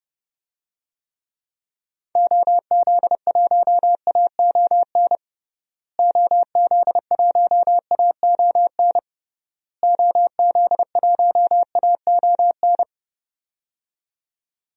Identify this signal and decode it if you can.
Morse code, heard as a keyed tone, about 30 wpm, 700 Hz. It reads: OZ1AON OZ1AON OZ1AON
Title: DXCC-100 Call Signs - 6 Characters - Repeated 3x - 30wpm